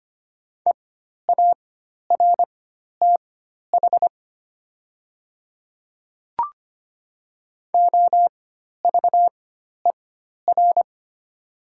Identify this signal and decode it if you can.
Morse code, heard as a keyed tone, about 25 wpm, spaced as for 12 wpm, 700 Hz. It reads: EARTH OVER
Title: 4x Character Spacing - Top 500 Words - 25wpm